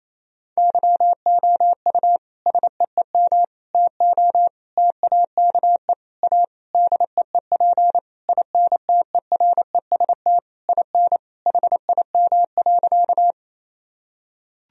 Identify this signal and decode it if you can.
Morse code, heard as a keyed tone, about 28 wpm, 700 Hz. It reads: YOU SEEM TO TAKE A DEEP INTEREST IN HIM.
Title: Sentences from Top 1000 Words - Review Part 3 - 28wpm